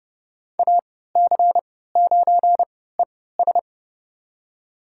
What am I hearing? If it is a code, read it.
Morse code, heard as a keyed tone, about 32 wpm, spaced as for 17 wpm, 700 Hz. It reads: AC9ES
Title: USA Call Signs - 3x Character Spacing - 30wpm